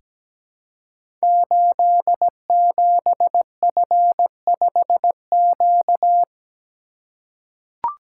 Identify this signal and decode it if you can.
Morse code, heard as a keyed tone, about 17 wpm, 700 Hz. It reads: 87F5Q
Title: Code Groups - 5 Letter:Numbers - 17wpm